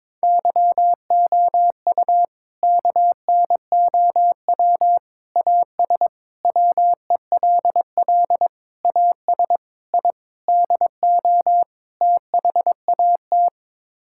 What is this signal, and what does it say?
Morse code, heard as a keyed tone, about 22 wpm, 700 Hz. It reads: YOU KNOW AS WELL AS I DO THAT
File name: practice / 22wpm / Sets of 8 Words - 22wpm